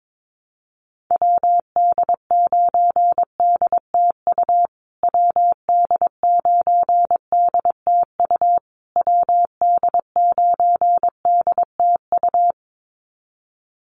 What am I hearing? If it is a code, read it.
Morse code, heard as a keyed tone, about 22 wpm, 700 Hz. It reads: WD9DTU WD9DTU WD9DTU
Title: USA Call Signs Repeated 3x 22wpm